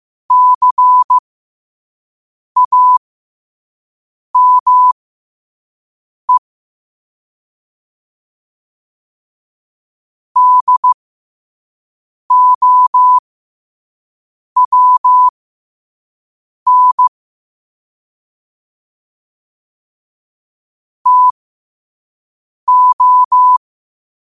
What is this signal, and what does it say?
Morse code, heard as a keyed tone, about 15 wpm, spaced as for 5 wpm, 1000 Hz. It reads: CAME DOWN TO
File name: 5wpm